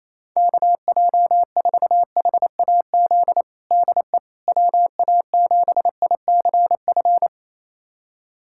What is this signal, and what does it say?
Morse code, heard as a keyed tone, about 28 wpm, 700 Hz. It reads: KJ4HAZ DE WA7ICF
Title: USA Callsign DE Callsign - 6 Characters - 28wpm